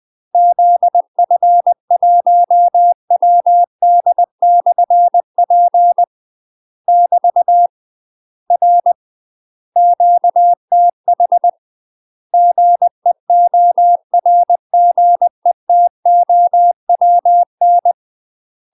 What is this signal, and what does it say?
Morse code, heard as a keyed tone, about 20 wpm, 700 Hz. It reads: ZF1WD/P <BT> R QTH GEORGETOWN